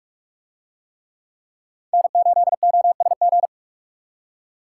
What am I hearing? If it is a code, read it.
Morse code, heard as a keyed tone, about 42 wpm, 700 Hz. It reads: N8OSG